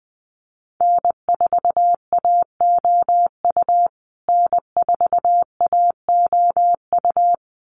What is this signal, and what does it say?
Morse code, heard as a keyed tone, about 20 wpm, 700 Hz. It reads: N4AOU N4AOU
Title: USA Call Signs Repeated 2x 20wpm